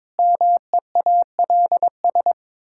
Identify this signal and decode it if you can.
Morse code, heard as a keyed tone, about 22 wpm, 700 Hz. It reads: MEALS
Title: Pangram Challenge v2 - 22wpm